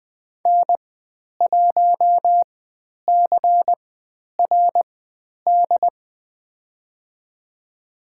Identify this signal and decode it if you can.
Morse code, heard as a keyed tone, about 20 wpm, spaced as for 10 wpm, 700 Hz. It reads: N1CRD
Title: 095 Lesson 16g - Letter c - Call Signs 20wpm